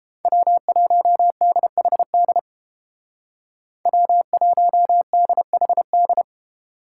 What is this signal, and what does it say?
Morse code, heard as a keyed tone, about 31 wpm, 700 Hz. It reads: W1DHD W1DHD